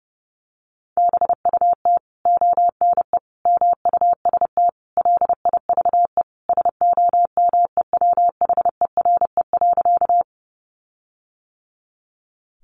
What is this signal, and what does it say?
Morse code, heard as a keyed tone, about 28 wpm, 700 Hz. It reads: BUT ONE MUST LIVE SOMEWHERE.
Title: Sentences from Top 1400 Words 30wpm